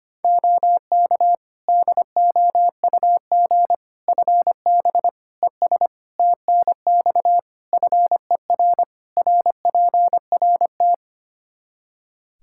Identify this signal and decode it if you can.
Morse code, heard as a keyed tone, about 25 wpm, 700 Hz. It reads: OK DOUG FB ES TNX FER RPRT
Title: Practice QSO - Exchange 2 - Info - 25wpm